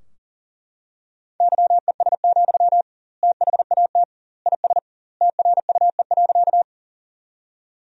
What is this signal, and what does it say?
Morse code, heard as a keyed tone, about 40 wpm, 700 Hz. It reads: YES, THAT IS TRUE.
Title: 2x Word-Spacing - Sentences from Top 500 Words - 40wpm